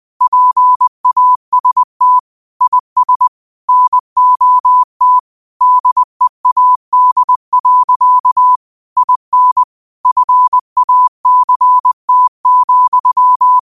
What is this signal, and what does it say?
Morse code, heard as a keyed tone, about 20 wpm, 1000 Hz. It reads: PAST IS NOT DEAD. IN FACT,